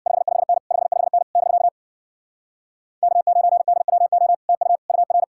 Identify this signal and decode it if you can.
Morse code, heard as a keyed tone, about 66 wpm, 700 Hz. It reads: 55N 55N <BK> K9DWO TU UR